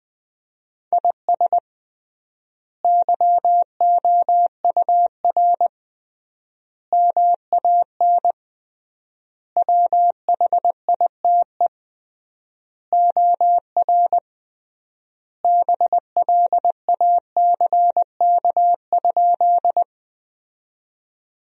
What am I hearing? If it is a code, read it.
Morse code, heard as a keyed tone, about 20 wpm, 700 Hz. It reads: IS YOUR MAN WHITE OR BLACK?
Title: 3x Word-Spacing - Sentences from Top 400 Words - 20wpm